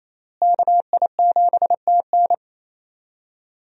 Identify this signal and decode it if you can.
Morse code, heard as a keyed tone, about 28 wpm, 700 Hz. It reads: KI7TN
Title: USA Call Signs - 5 Characters - Repeated 2x - 28wpm